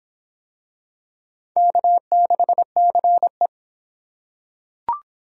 Morse code, heard as a keyed tone, about 22 wpm, 700 Hz. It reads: K6CE